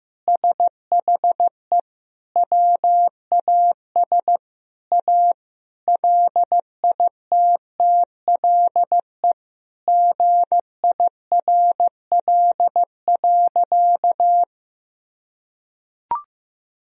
Morse code, heard as a keyed tone, about 16 wpm, 700 Hz. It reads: SHE WAS A LITTLE GIRL.
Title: Sentences from Top 1000 Words - Review Part 1 - 15wpm